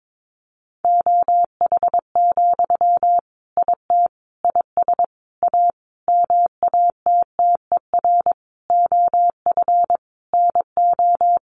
Morse code, heard as a keyed tone, about 22 wpm, 700 Hz. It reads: OH, IT IS A MATTER OF NO
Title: Sentences from Top 1200 Words 22wpm